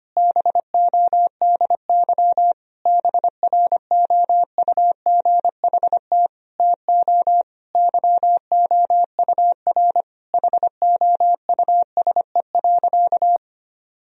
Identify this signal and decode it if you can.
Morse code, heard as a keyed tone, about 25 wpm, 700 Hz. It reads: BODY BROUGHT TO YOUR HOUSE.
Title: Sentences from Top 1000 Words - Review Part 2 - 25wpm